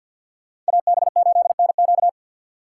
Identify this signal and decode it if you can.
Morse code, heard as a keyed tone, about 64 wpm, 700 Hz. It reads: AB8NQ